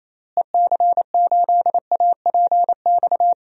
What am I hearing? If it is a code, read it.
Morse code, heard as a keyed tone, about 28 wpm, 700 Hz. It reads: EC8APX
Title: DXCC-100 Call Signs - 6 Characters - Repeated 3x - 28wpm